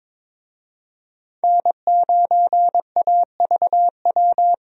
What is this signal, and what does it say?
Morse code, heard as a keyed tone, about 22 wpm, 700 Hz. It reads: N9AVW
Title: USA Call Signs - 5 Characters - 22wpm